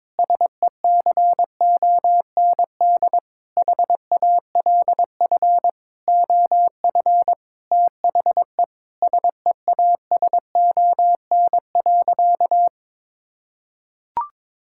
Morse code, heard as a keyed tone, about 22 wpm, 700 Hz. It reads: SECOND HALF OF THE SEASON.
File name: Sentences with 7-Word Phrases - 22wpm